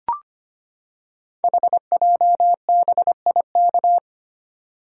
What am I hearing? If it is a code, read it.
Morse code, heard as a keyed tone, about 25 wpm, 700 Hz. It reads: HJBIK
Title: Code Groups - 5 Letters - 25wpm